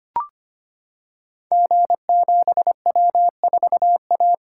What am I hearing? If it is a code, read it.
Morse code, heard as a keyed tone, about 25 wpm, 700 Hz. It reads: G7W4A
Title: Code Groups - 5 Letter:Numbers - 25wpm